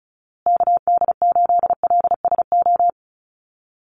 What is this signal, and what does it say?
Morse code, heard as a keyed tone, about 34 wpm, 700 Hz. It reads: KD8LSO
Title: USA Call Signs Repeated 2x 35wpm